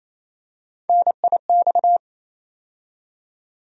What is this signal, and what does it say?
Morse code, heard as a keyed tone, about 28 wpm, 700 Hz. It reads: NIX
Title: USA Call Sign Postfixes - 3 Letters 28wpm